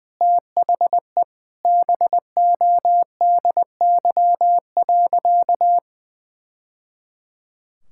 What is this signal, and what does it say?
Morse code, heard as a keyed tone, about 20 wpm, 700 Hz. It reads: THE BODY.
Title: Sentences from Top 1000 Words - Review Part 6 - 20wpm